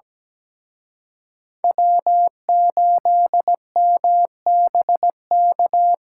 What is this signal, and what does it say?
Morse code, heard as a keyed tone, about 20 wpm, 700 Hz. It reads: W8MBK
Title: Speed Racing - USA 1x3 Call Signs 17wpm